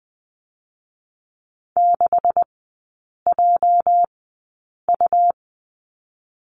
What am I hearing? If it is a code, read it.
Morse code, heard as a keyed tone, about 20 wpm, spaced as for 8 wpm, 700 Hz. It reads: 6JU
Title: 218 Lesson 35e - Number 0 - Three Characters 20wpm